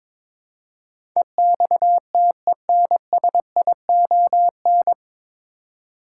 Morse code, heard as a keyed tone, about 22 wpm, 700 Hz. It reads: EXTENSION
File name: 9 Letter Words - First 3 Letters Given - 22wpm